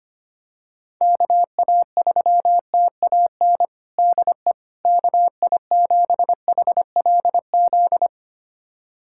Morse code, heard as a keyed tone, about 25 wpm, 700 Hz. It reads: KA3TAN DE KI7HLZ
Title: USA Callsign DE Callsign - 6 Characters - 25wpm